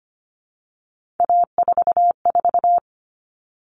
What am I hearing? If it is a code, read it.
Morse code, heard as a keyed tone, about 25 wpm, 700 Hz. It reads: A44
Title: All Cut Numbers - 3-Digit 25wpm